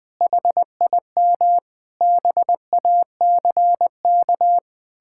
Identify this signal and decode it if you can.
Morse code, heard as a keyed tone, about 20 wpm, 700 Hz. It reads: HIM BACK
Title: Sentences from Top 1000 Words - Review Part 6 - 20wpm